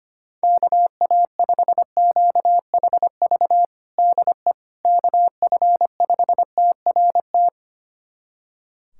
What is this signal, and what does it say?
Morse code, heard as a keyed tone, about 25 wpm, 700 Hz. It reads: KA5QHV DE KF5TRT
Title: USA Callsign DE Callsign - 6 Characters - 25wpm